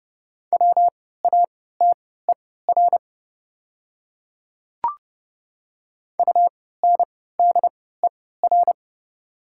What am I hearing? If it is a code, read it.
Morse code, heard as a keyed tone, about 30 wpm, spaced as for 17 wpm, 700 Hz. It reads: WATER UNDER
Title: 3x Character Spacing - Top 300 Words - 30wpm